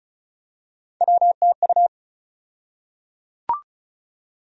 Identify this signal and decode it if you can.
Morse code, heard as a keyed tone, about 36 wpm, 700 Hz. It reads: WTU
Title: USA Call Sign Postfixes - 3 Letters 35wpm